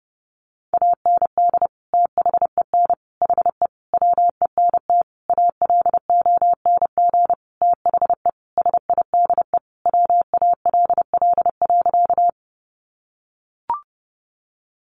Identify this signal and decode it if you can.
Morse code, heard as a keyed tone, about 30 wpm, 700 Hz. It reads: AND THEN HE WENT ALONG THE SIDE WALL.
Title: Sentences from Top 1100 Words 30wpm